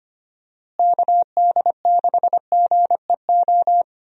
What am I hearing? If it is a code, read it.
Morse code, heard as a keyed tone, about 25 wpm, 700 Hz. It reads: KD6GEO
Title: USA Call Signs - 6 Characters - Repeated 2x - 25wpm